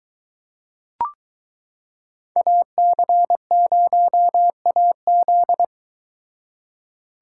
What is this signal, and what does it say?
Morse code, heard as a keyed tone, about 22 wpm, 700 Hz. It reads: AC0AZ